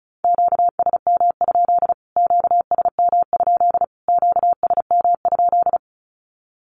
Morse code, heard as a keyed tone, about 34 wpm, 700 Hz. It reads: QSM? QSM? QSM?
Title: Q-Codes Repeated 3x 35wpm